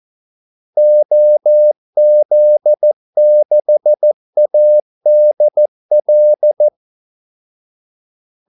Morse code, heard as a keyed tone, about 14 wpm, 600 Hz. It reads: OZ6ADL
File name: Callsigns-14wpm